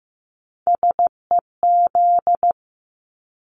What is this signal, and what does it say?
Morse code, heard as a keyed tone, about 15 wpm, 700 Hz. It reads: SEZ
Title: Abbreviations Repeated 3x - 15wpm